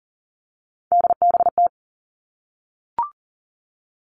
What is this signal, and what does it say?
Morse code, heard as a keyed tone, about 40 wpm, 700 Hz. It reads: DBT
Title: All Cut Numbers - 3-Digit 40wpm